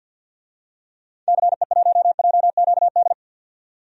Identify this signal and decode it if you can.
Morse code, heard as a keyed tone, about 48 wpm, 700 Hz. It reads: CE1JQD
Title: DXCC-100 Call Signs - 6 Characters - 50wpm